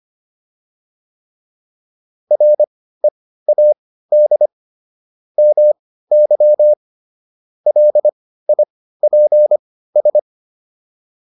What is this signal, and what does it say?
Morse code, heard as a keyed tone, about 25 wpm, spaced as for 15 wpm, 600 Hz. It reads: READ MY LIPS